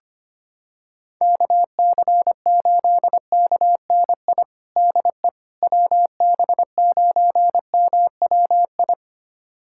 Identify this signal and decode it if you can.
Morse code, heard as a keyed tone, about 25 wpm, 700 Hz. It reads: KC8KNI DE WB9MWI
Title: USA Callsign DE Callsign 25wpm